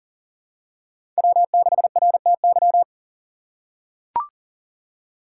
Morse code, heard as a keyed tone, about 38 wpm, 700 Hz. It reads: W6RTY
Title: USA Call Signs - 5 Characters - 3x Character Spacing - 40wpm